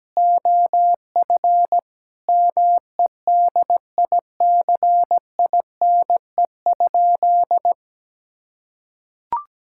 Morse code, heard as a keyed tone, about 17 wpm, 700 Hz. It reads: OF MEDICINE?
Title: Sentences from Top 1000 Words - Review Part 6 - 17wpm